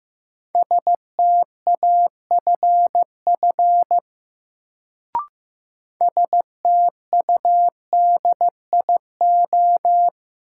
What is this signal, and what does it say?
Morse code, heard as a keyed tone, about 15 wpm, 700 Hz. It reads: STAFF STUDIO
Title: Sentences from Top 1200 Words - New Word Drill Alt - 15wpm